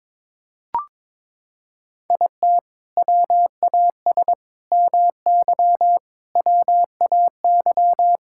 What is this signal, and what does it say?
Morse code, heard as a keyed tone, about 22 wpm, 700 Hz. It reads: IT WAS MY WAY
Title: Sentences from Top 1600 Words 22wpm